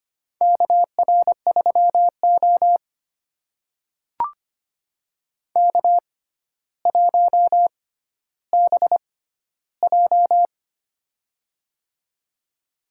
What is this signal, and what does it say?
Morse code, heard as a keyed tone, about 25 wpm, spaced as for 13 wpm, 700 Hz. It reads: KR3O K1BJ
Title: USA Call Signs - 4 Characters - 6x Character Spacing - 25wpm